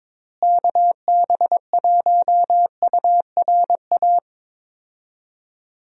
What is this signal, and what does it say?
Morse code, heard as a keyed tone, about 22 wpm, 700 Hz. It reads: KB1URA
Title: USA Call Signs - 6 Characters - Repeated 3x - 22wpm